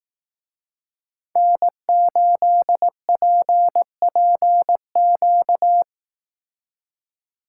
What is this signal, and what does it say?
Morse code, heard as a keyed tone, about 18 wpm, 700 Hz. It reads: N8PPQ
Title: Speed Racing - USA 1x3 Call Signs 15wpm